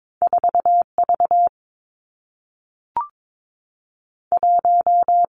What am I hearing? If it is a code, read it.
Morse code, heard as a keyed tone, about 22 wpm, 700 Hz. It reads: 4 V 1
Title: Two Challenging Letter-Number Combinations 22wpm